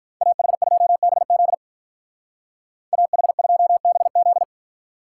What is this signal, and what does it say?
Morse code, heard as a keyed tone, about 43 wpm, 700 Hz. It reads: AH2BZ AH2BZ
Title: Speed Racing - USA 2x2 Call Signs 35wpm